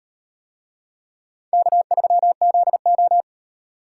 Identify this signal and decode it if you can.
Morse code, heard as a keyed tone, about 38 wpm, 700 Hz. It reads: K3ZO